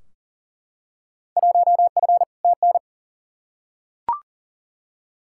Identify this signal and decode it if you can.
Morse code, heard as a keyed tone, about 39 wpm, 700 Hz. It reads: 1F TN
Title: Field Day Exchanges - 40wpm